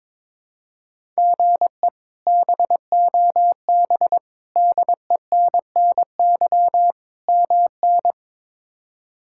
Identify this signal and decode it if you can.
Morse code, heard as a keyed tone, about 22 wpm, 700 Hz. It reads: GE BOB DENNY MN
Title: K1USN SST QSO Practice - 22wpm